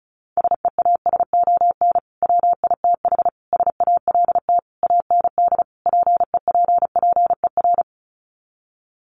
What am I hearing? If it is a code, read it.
Morse code, heard as a keyed tone, about 34 wpm, 700 Hz. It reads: SEASON WITH SALT AND PEPPER
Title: Sets of 5 Words 35wpm